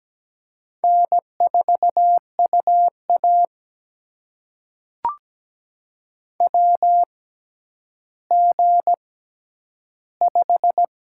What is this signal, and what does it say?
Morse code, heard as a keyed tone, about 17 wpm, spaced as for 10 wpm, 700 Hz. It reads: N4UA WG5
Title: USA Call Signs - 4 Characters - 6x Character Spacing - 17wpm